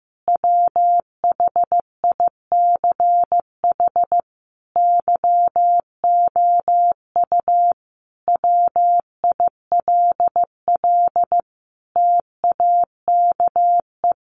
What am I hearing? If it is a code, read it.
Morse code, heard as a keyed tone, about 14 wpm, 700 Hz. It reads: WHICH YOU WILL TAKE
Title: Sentences from Top 1400 Words 15wpm